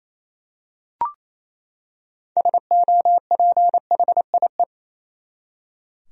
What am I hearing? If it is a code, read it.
Morse code, heard as a keyed tone, about 28 wpm, 700 Hz. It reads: SOPHIE
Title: USA Common Names - 28wpm